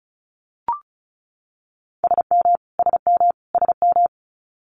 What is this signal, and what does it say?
Morse code, heard as a keyed tone, about 36 wpm, 700 Hz. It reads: SM SM SM
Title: Multi-letter Phonemes Repeated 3x 35wpm